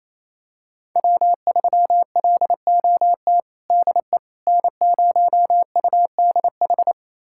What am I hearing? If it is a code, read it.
Morse code, heard as a keyed tone, about 28 wpm, 700 Hz. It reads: W3LOT DE N0UDH
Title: USA Callsign DE Callsign - 5 Characters - 28wpm